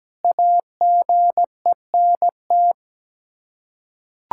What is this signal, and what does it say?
Morse code, heard as a keyed tone, about 17 wpm, 700 Hz. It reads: AGENT
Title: K1USN SST QSO Practice 17wpm